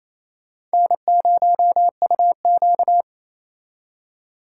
Morse code, heard as a keyed tone, about 36 wpm, 700 Hz. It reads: N0UQ